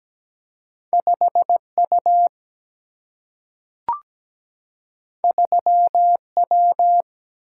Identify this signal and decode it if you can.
Morse code, heard as a keyed tone, about 17 wpm, 700 Hz. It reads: 5U 3W
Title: Single-Letter Single-Number 17wpm